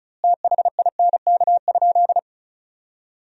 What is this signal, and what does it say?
Morse code, heard as a keyed tone, about 36 wpm, 700 Hz. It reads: THINK?
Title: 2x Word-Spacing - Sentences from Top 200 Words - 35wpm